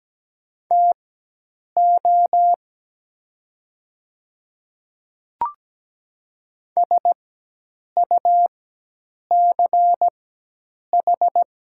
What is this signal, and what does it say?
Morse code, heard as a keyed tone, about 17 wpm, spaced as for 8 wpm, 700 Hz. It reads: TO SUCH
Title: 4x Character Spacing - Top 200 Words - 17wpm